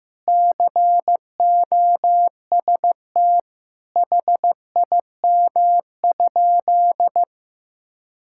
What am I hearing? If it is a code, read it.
Morse code, heard as a keyed tone, about 15 wpm, 700 Hz. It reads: COST HIM?
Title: Sentences from Top 1000 Words - Review Part 5 - 15wpm